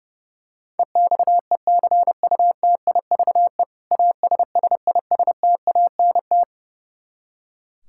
Morse code, heard as a keyed tone, about 32 wpm, 700 Hz. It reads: EXECUTIVE ASSISTANT
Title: Practice QSO - Exchange 3 - More Info - 30wpm